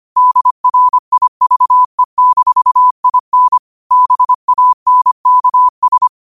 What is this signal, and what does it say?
Morse code, heard as a keyed tone, about 25 wpm, 1000 Hz. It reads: DRIVE-IN BANKS
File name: qotdfiles / 25wpm